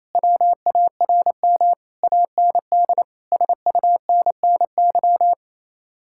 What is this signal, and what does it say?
Morse code, heard as a keyed tone, about 38 wpm, 700 Hz. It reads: WARM AND SUNNY